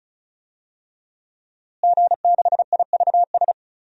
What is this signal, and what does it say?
Morse code, heard as a keyed tone, about 36 wpm, 700 Hz. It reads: G6IVS
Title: DXCC-100 Call Signs - 5 Characters - Repeated 3x - 35wpm